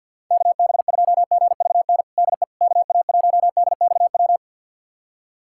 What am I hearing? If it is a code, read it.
Morse code, heard as a keyed tone, about 47 wpm, 700 Hz. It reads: KB2GVN DE KA1DXW
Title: USA Callsign DE Callsign - 6 Characters - 50wpm